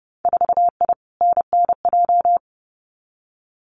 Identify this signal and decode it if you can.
Morse code, heard as a keyed tone, about 29 wpm, 700 Hz. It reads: 4I NNJ
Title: Winter Field Day Exchanges 30wpm